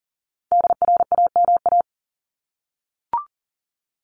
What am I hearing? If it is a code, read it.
Morse code, heard as a keyed tone, about 40 wpm, 700 Hz. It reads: DRAMA
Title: Top 1251-1500 Words 40wpm